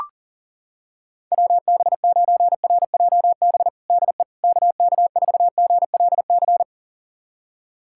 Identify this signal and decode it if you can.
Morse code, heard as a keyed tone, about 40 wpm, 700 Hz. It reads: WB9RJB DE KK4GLC
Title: USA Callsign DE Callsign 40wpm